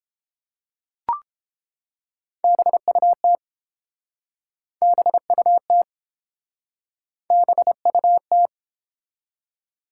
Morse code, heard as a keyed tone, about 30 wpm, 700 Hz. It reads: BUT BUT BUT